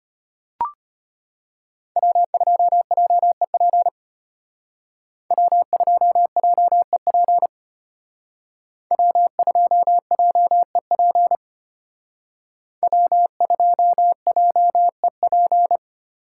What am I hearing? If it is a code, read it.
Morse code, heard as a keyed tone, about 30 wpm, 700 Hz. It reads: W2JEP W2JEP W2JEP W2JEP
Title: Speed Racing - USA 1x3 Call Signs 25wpm